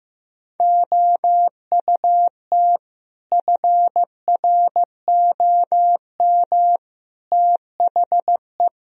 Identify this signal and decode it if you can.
Morse code, heard as a keyed tone, about 15 wpm, 700 Hz. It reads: OUT FROM THE
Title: Sentences from Top 1000 Words - Review Part 3 - 15wpm